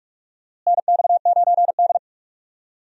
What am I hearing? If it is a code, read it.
Morse code, heard as a keyed tone, about 46 wpm, 700 Hz. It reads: NX9D
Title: USA Call Signs - 4 Characters - Repeated 2x - 45wpm